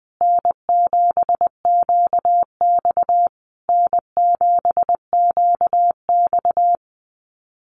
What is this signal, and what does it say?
Morse code, heard as a keyed tone, about 20 wpm, 700 Hz. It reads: N7QX N7QX
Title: USA Call Signs Repeated 3x 20wpm